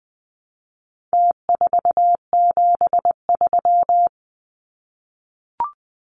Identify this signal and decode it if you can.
Morse code, heard as a keyed tone, about 20 wpm, 700 Hz. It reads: T473
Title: Numbers - Contest Style - 4-Digit 20wpm